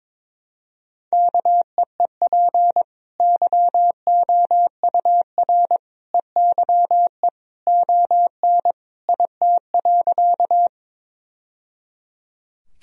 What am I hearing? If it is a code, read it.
Morse code, heard as a keyed tone, about 22 wpm, 700 Hz. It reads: KEEP YOUR EYE ON IT.
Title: Sentences from Top 1000 Words - Review Part 1 - 22wpm